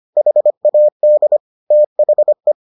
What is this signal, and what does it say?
Morse code, heard as a keyed tone, about 25 wpm, 600 Hz. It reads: HAD THE